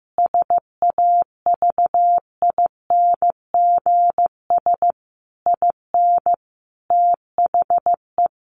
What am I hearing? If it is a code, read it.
Morse code, heard as a keyed tone, about 15 wpm, 700 Hz. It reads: SAVINGS IN THE
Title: Sentences from Top 1200 Words 15wpm